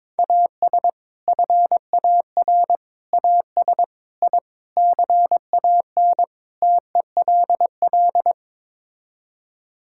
Morse code, heard as a keyed tone, about 21 wpm, 700 Hz. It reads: AS FAR AS I CAN TELL
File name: practice / 22wpm / Sets of 6 Words - 22wpm